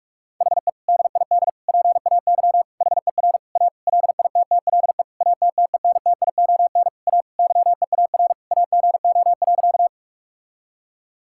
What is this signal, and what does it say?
Morse code, heard as a keyed tone, about 46 wpm, 700 Hz. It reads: HE DID PAY HER A LITTLE ATTENTION A YEAR AGO.
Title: Sentences from Top 1000 Words - Review Part 6 - 45wpm